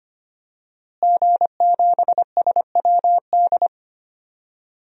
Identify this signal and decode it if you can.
Morse code, heard as a keyed tone, about 25 wpm, 700 Hz. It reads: G7SWD
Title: DXCC-100 Call Signs - 5 Characters - Repeated 2x - 25wpm